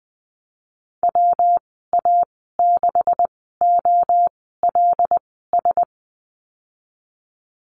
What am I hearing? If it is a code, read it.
Morse code, heard as a keyed tone, about 20 wpm, spaced as for 15 wpm, 700 Hz. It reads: WA6OLS
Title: 161 Lesson 26g - Letter g - Call Signs 20wpm